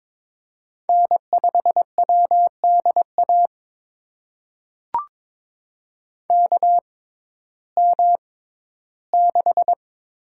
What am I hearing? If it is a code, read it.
Morse code, heard as a keyed tone, about 22 wpm, spaced as for 13 wpm, 700 Hz. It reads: N5WDA KM6
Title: USA Call Signs - 6x Character Spacing - 22wpm